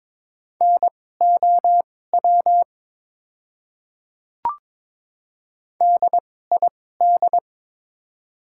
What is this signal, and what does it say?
Morse code, heard as a keyed tone, about 23 wpm, spaced as for 16 wpm, 690 Hz. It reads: NOW DID
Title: 2x Character Spacing - Top 100 Words - 22wpm